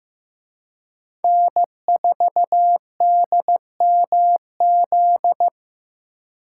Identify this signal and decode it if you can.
Morse code, heard as a keyed tone, about 18 wpm, 700 Hz. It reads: N4DMZ
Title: Speed Racing - USA 1x3 Call Signs 15wpm